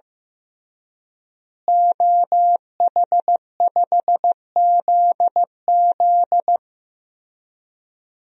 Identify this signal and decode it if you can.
Morse code, heard as a keyed tone, about 15 wpm, 700 Hz. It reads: OH5ZZ
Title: DXCC-100 Call Signs - 5 Characters - Repeated 3x - 15wpm